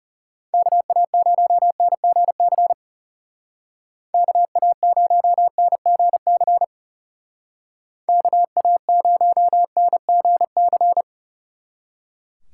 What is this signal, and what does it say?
Morse code, heard as a keyed tone, about 36 wpm, 700 Hz. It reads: KA0NGC KA0NGC KA0NGC
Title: Speed Racing - USA Call Signs 30wpm